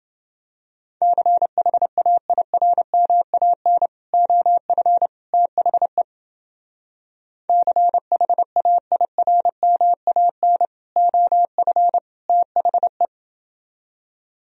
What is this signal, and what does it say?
Morse code, heard as a keyed tone, about 29 wpm, 700 Hz. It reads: CHAIRMAN OF THE CHAIRMAN OF THE